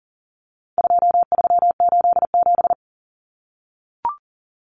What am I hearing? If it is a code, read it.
Morse code, heard as a keyed tone, about 40 wpm, 700 Hz. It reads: 2387
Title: Common Cut Numbers - 4-Digit 40wpm